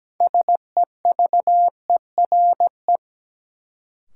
Morse code, heard as a keyed tone, about 17 wpm, 700 Hz. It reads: SEVERE
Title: Sentences from Top 1400 Words - New Word Drill - 17wpm